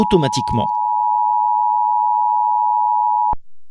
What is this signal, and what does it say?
Emergency Alert System (SAME) tones (the 853 and 960 Hz attention tone), with speech. It is heard from 0.6 to 3.4 s.